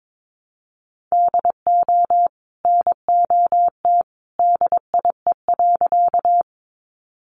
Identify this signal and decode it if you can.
Morse code, heard as a keyed tone, about 22 wpm, 700 Hz. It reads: DO NOT DIE.
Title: Sentences from Top 1300 Words 22wpm